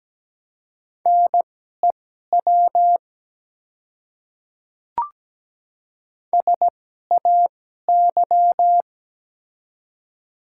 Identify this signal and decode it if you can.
Morse code, heard as a keyed tone, about 17 wpm, spaced as for 12 wpm, 700 Hz. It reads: NEW SAY
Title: 2x Character Spacing - Top 200 Words - 17wpm